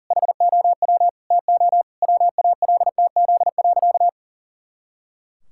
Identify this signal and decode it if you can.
Morse code, heard as a keyed tone, about 40 wpm, 700 Hz. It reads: HOW TO WALTZ.
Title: Pangram Challenge v2 - 40wpm